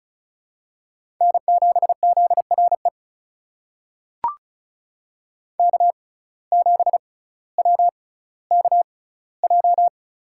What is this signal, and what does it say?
Morse code, heard as a keyed tone, about 36 wpm, spaced as for 18 wpm, 700 Hz. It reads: N7ZRE K7WKJ
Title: USA Call Signs - 5 Characters - 6x Character Spacing - 35wpm